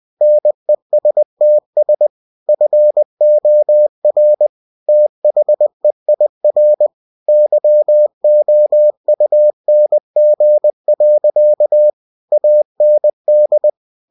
Morse code, heard as a keyed tone, about 20 wpm, 600 Hz. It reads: NESTS FOR THEIR YOUNG. AND
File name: cw-books / winnie-the-pooh / winnie-20wpm-page-0094